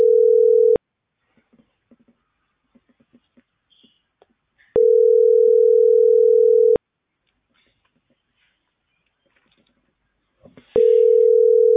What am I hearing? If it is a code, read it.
Ringback tone (North America), 440 + 480 Hz.